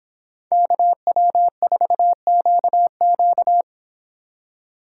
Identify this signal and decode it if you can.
Morse code, heard as a keyed tone, about 25 wpm, 700 Hz. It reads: KW4QQ